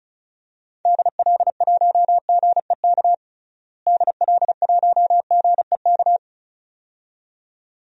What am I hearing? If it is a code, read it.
Morse code, heard as a keyed tone, about 34 wpm, 700 Hz. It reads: DL1GEK DL1GEK
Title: DXCC-100 Call Signs - 6 Characters - Repeated 2x - 35wpm